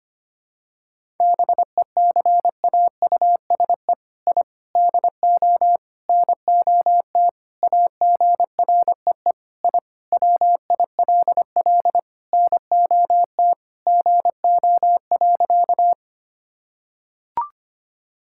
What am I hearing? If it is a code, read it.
Morse code, heard as a keyed tone, about 25 wpm, 700 Hz. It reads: BECAUSE I DO NOT AGREE I WILL NOT GO.
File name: practice / 25wpm / Sentences with 4-Word Phrases - 25wpm